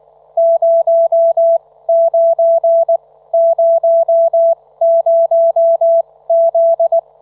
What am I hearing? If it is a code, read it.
Morse code, heard as a keyed tone, about 19 wpm, spaced as for 15 wpm, 670 Hz, with radio noise. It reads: 0900Z
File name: Trivia